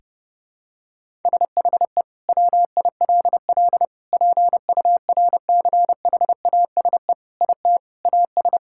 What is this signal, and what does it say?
Morse code, heard as a keyed tone, about 30 wpm, 700 Hz. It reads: SHE WILL PURCHASE IT AS
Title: Sentences from Top 1000 Words - Review Part 5 - 30wpm